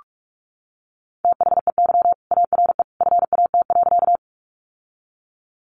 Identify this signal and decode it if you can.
Morse code, heard as a keyed tone, about 44 wpm, 700 Hz. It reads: THEY ARE FAT.
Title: Sentences from Top 600 Words 45wpm